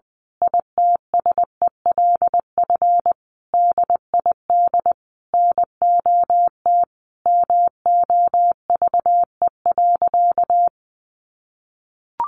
Morse code, heard as a keyed tone, about 19 wpm, 700 Hz. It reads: ITSELF DID NOT MOVE.
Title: A Sound Of Thunder 20wpm